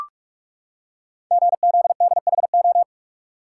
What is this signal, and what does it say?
Morse code, heard as a keyed tone, about 44 wpm, 700 Hz. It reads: GZDHO
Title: Code Groups - 5 Letters - 45wpm